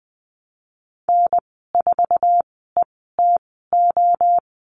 Morse code, heard as a keyed tone, about 20 wpm, spaced as for 15 wpm, 700 Hz. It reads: N4ETO
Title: 055 Lesson 10g - Letter r - Call Signs 20wpm